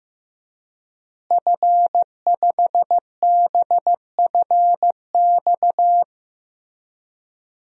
Morse code, heard as a keyed tone, about 15 wpm, 700 Hz. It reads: F5BFX
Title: DXCC-100 Call Signs - 5 Characters - Repeated 3x - 15wpm